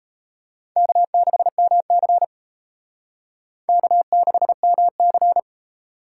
Morse code, heard as a keyed tone, about 31 wpm, 700 Hz. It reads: K6MC K6MC
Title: Speed Racing - USA Call Signs 28wpm